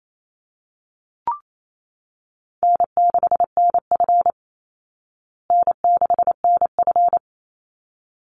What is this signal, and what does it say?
Morse code, heard as a keyed tone, about 28 wpm, 700 Hz. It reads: N6NF N6NF
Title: Easy Call Signs Repeated 2x - 28wpm